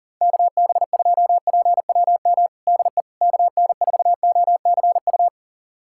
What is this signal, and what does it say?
Morse code, heard as a keyed tone, about 37 wpm, 700 Hz. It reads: KB2PWM DE KN4OCU
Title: USA Callsign DE Callsign - 6 Characters - 40wpm